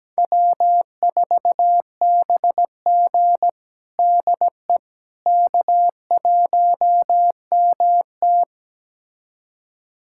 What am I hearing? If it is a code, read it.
Morse code, heard as a keyed tone, about 17 wpm, 700 Hz. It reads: W4BG DE K1MT
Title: USA Callsign DE Callsign - 4 Characters - 17wpm